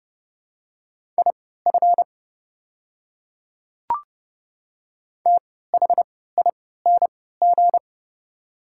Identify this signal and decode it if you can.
Morse code, heard as a keyed tone, about 30 wpm, spaced as for 17 wpm, 700 Hz. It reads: IF THING
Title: 3x Character Spacing - Top 200 Words - 30wpm